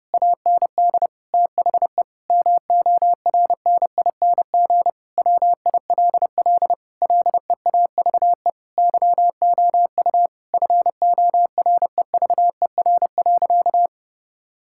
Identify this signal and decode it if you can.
Morse code, heard as a keyed tone, about 30 wpm, 700 Hz. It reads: AND THE MORNING WILL LEAVE YOU FOREVER.
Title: Sentences from Top 1600 Words 30wpm